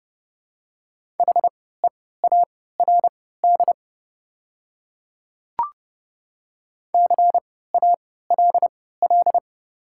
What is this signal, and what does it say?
Morse code, heard as a keyed tone, about 30 wpm, spaced as for 17 wpm, 690 Hz. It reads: HEARD CALL
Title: 3x Character Spacing - Top 500 Words - 30wpm